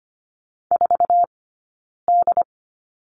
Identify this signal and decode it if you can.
Morse code, heard as a keyed tone, about 25 wpm, 700 Hz. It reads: 4 D
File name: course / 25:10wpm / 092 Lesson 16d - Letter c - Two Characters 25wpm